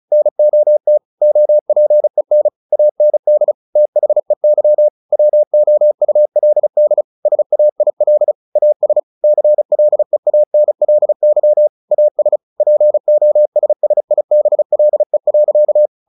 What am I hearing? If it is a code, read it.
Morse code, heard as a keyed tone, about 34 wpm, 600 Hz. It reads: NOT OPEN AND THEY WOULD SAIL AS CLEANLY AS POSSIBLE.
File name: the old man and the sea 2 35wpm